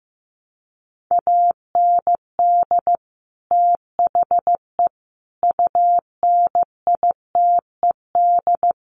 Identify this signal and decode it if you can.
Morse code, heard as a keyed tone, about 15 wpm, 700 Hz. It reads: AND THE UNITED
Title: Sets of 4 Words 15wpm